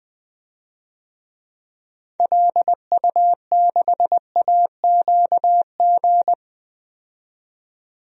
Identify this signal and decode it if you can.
Morse code, heard as a keyed tone, about 20 wpm, 700 Hz. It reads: LU6AQG
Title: DXCC-100 Call Signs - 6 Characters - 20wpm